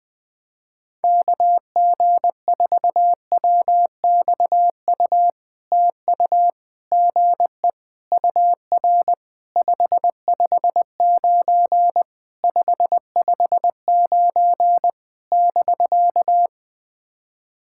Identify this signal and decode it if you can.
Morse code, heard as a keyed tone, about 20 wpm, 700 Hz. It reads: KG4WXU TU GE UR 559 559 <BK>
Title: POTA QSO Practice - 20wpm